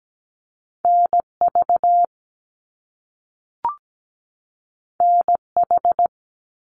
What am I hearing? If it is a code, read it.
Morse code, heard as a keyed tone, about 17 wpm, 700 Hz. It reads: NV NH
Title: US States 17wpm